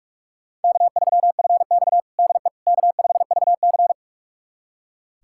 Morse code, heard as a keyed tone, about 42 wpm, 700 Hz. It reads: K3FX DE K5VC
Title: USA Callsign DE Callsign - 4 Characters - 45wpm